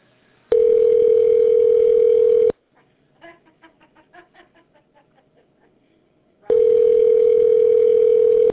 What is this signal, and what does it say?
Ringback tone (North America), 440 + 480 Hz.